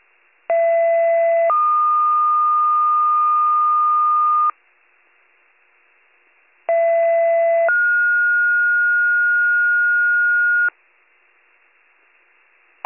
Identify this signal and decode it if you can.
Two-tone page, heard as a one-second tone, then a three-second tone, about 670 then 1220 Hz, about 670 then 1430 Hz. It is heard from 0.5 to 10.7 s.